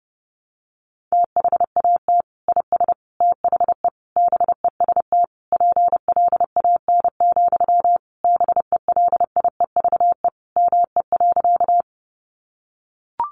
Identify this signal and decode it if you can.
Morse code, heard as a keyed tone, about 30 wpm, 700 Hz. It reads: THAT IS THE BEST PLAN, BELIEVE ME.
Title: Sentences from Top 1000 Words 30wpm